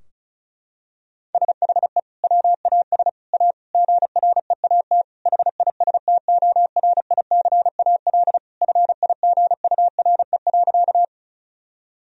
Morse code, heard as a keyed tone, about 36 wpm, 700 Hz. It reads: SHE WAS A GREAT HISTORICAL FIGURE.
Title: Sentences from Top 1000 Words - Review Part 4 - 35wpm